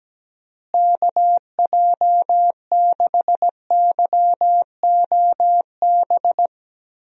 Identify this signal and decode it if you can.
Morse code, heard as a keyed tone, about 17 wpm, 700 Hz. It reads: KJ6YOB
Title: USA Call Signs - 6 Characters - Repeated 2x - 17wpm